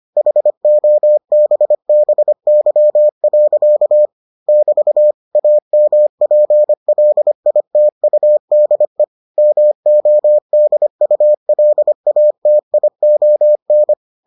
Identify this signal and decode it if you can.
Morse code, heard as a keyed tone, about 25 wpm, 600 Hz. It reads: HOBBY. <BT> AMPLITUDE MODULATION